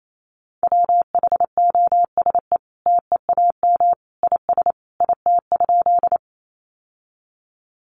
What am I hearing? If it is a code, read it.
Morse code, heard as a keyed tone, about 28 wpm, 700 Hz. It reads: WHOSE TEAM IS IT?
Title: Sentences from Top 1500 Words 28wpm